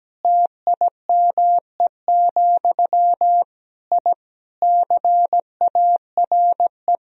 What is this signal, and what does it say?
Morse code, heard as a keyed tone, about 17 wpm, 700 Hz. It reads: TIME, I CARE
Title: Sentences from Top 1000 Words - Review Part 2 - 17wpm